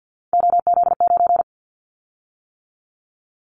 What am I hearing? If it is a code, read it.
Morse code, heard as a keyed tone, about 50 wpm, 700 Hz. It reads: OZ9
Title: Callsign Prefixes for Top 100 DXCC - 50wpm